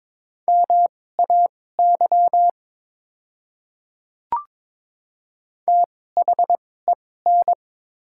Morse code, heard as a keyed tone, about 22 wpm, spaced as for 16 wpm, 700 Hz. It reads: MAY THEN
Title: 2x Character Spacing - Top 100 Words - 22wpm